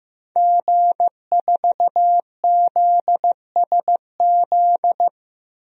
Morse code, heard as a keyed tone, about 15 wpm, 700 Hz. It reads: G4ZSZ
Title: DXCC-100 Call Signs - 5 Characters - Repeated 3x - 15wpm